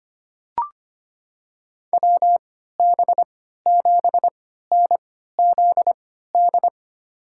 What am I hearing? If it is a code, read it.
Morse code, heard as a keyed tone, about 25 wpm, spaced as for 14 wpm, 700 Hz. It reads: WB7NZD
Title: USA Call Signs - 6 Characters - 3x Character Spacing - 25wpm